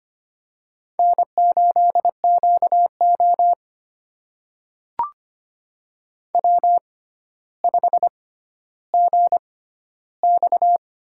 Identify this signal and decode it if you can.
Morse code, heard as a keyed tone, about 25 wpm, spaced as for 13 wpm, 700 Hz. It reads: N8QO W5GX
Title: USA Call Signs - 4 Characters - 6x Character Spacing - 25wpm